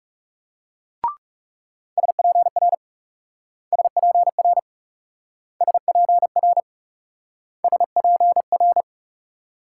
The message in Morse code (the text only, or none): SPR SPR SPR SPR